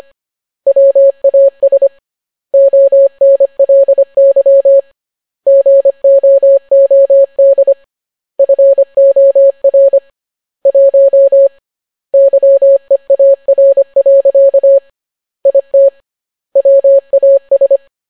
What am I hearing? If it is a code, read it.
Morse code, heard as a keyed tone, about 25 wpm, 550 Hz. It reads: WAS ONLY GOOD FOR 1 YEAR. IT WAS